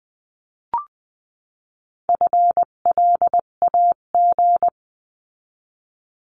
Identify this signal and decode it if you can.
Morse code, heard as a keyed tone, about 20 wpm, 700 Hz. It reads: FLAG
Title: 258 Lesson 42c - Words 16-20wpm